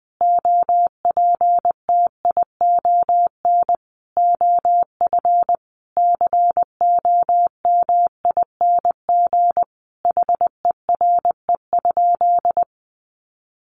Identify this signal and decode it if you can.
Morse code, heard as a keyed tone, about 20 wpm, 700 Hz. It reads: OPTION OF COMING HERE?
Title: Sentences from Top 1300 Words 20wpm